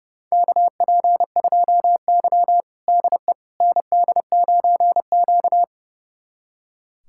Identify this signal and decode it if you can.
Morse code, heard as a keyed tone, about 30 wpm, 700 Hz. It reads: KP2Y DE ND9Q
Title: USA Callsign DE Callsign - 4 Characters - 30wpm